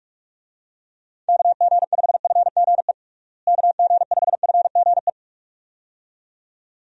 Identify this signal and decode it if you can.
Morse code, heard as a keyed tone, about 42 wpm, 700 Hz. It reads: KG5FGE KG5FGE
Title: USA Call Signs - 6 Characters - Repeated 2x - 45wpm